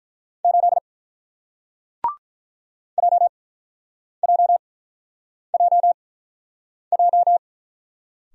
Morse code, heard as a keyed tone, about 48 wpm, spaced as for 9 wpm, 700 Hz. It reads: 8 JJJJ